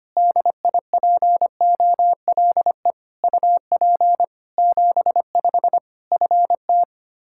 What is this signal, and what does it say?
Morse code, heard as a keyed tone, about 25 wpm, 700 Hz. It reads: DIPOLE UP 75 FT
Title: Practice QSO - Exchange 2 - Info - 25wpm